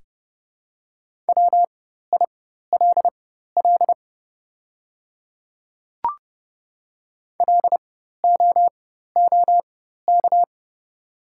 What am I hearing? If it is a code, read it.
Morse code, heard as a keyed tone, about 30 wpm, spaced as for 14 wpm, 690 Hz. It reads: WILL LOOK
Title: 4x Character Spacing - Top 100 Words - 30wpm